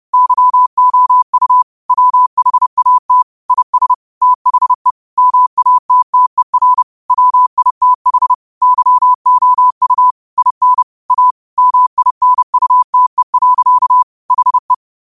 Morse code, heard as a keyed tone, about 30 wpm, 1000 Hz. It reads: YOU WHAT IS THE MATTER WITH YOU IN A MINUTE. HE
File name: qotdfiles / 30wpm